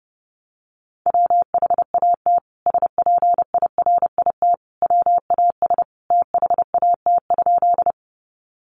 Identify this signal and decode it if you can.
Morse code, heard as a keyed tone, about 30 wpm, 700 Hz. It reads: WHAT SPIRIT WAS THAT?
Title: Sentences from Top 1400 Words 30wpm